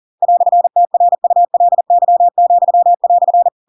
Morse code, heard as a keyed tone, about 40 wpm, 700 Hz. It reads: "TRULY,"
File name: Pinocchio / Pinocchio ch8 40 40 ik1hge